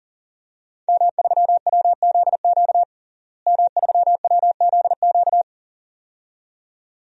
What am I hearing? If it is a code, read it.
Morse code, heard as a keyed tone, about 40 wpm, 700 Hz. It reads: M3WZQ M3WZQ
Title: DXCC-100 Call Signs - 5 Characters - Repeated 2x - 40wpm